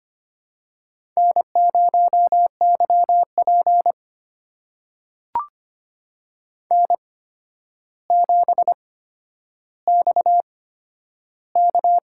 Morse code, heard as a keyed tone, about 25 wpm, spaced as for 11 wpm, 700 Hz. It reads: N0YP N7XK
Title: USA Call Signs - 4 Characters - 8x Character Spacing - 25wpm